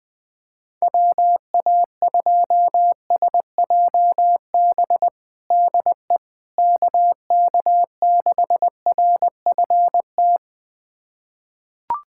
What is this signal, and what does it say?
Morse code, heard as a keyed tone, about 20 wpm, 700 Hz. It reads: WA2SJB DE KK6RFT
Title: USA Callsign DE Callsign - 6 Characters - 20wpm